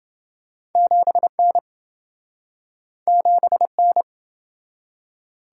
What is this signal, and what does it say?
Morse code, heard as a keyed tone, about 27 wpm, 700 Hz. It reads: 7N 7N